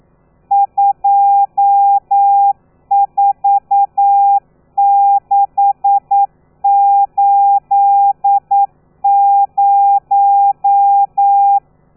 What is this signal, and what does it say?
Morse code, heard as a keyed tone, about 9 wpm, 800 Hz. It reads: 24680
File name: telegrafia / i2tfj / Lez13 40